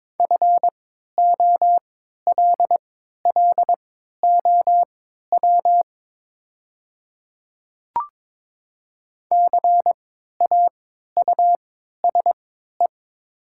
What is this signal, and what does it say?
Morse code, heard as a keyed tone, about 21 wpm, spaced as for 12 wpm, 700 Hz. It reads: FOLLOW CAUSE
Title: 3x Character Spacing - Top 300 Words - 22wpm